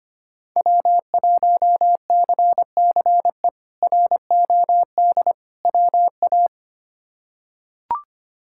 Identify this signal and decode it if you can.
Morse code, heard as a keyed tone, about 25 wpm, 700 Hz. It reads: W1CCE ROD WA
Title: K1USN SST QSO Practice - 25wpm